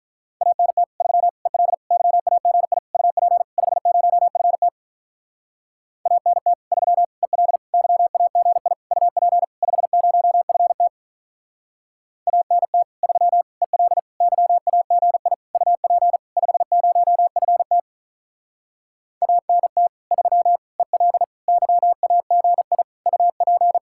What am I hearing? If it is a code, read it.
Morse code, heard as a keyed tone, about 46 wpm, 700 Hz. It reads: ANT 3 EL YAGI UP 50FT ANT 3 EL YAGI UP 50FT ANT 3 EL YAGI UP 50FT ANT 3 EL YAGI UP